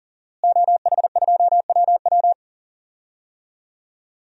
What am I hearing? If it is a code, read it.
Morse code, heard as a keyed tone, about 38 wpm, 700 Hz. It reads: OH2WW
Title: DXCC-100 Call Signs - 5 Characters - Repeated 3x - 40wpm